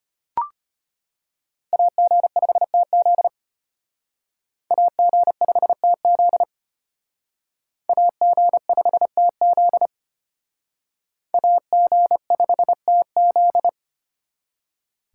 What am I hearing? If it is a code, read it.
Morse code, heard as a keyed tone, about 32 wpm, 700 Hz. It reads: AG5TZ AG5TZ AG5TZ AG5TZ